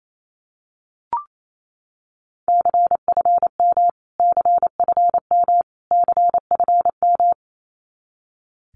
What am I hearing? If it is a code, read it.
Morse code, heard as a keyed tone, about 28 wpm, 700 Hz. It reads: CFM CFM CFM
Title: Abbreviations Repeated 3x - 28wpm